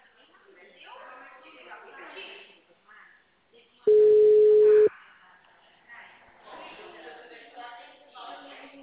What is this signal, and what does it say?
Ringback tone (Europe, 425 Hz), with speech.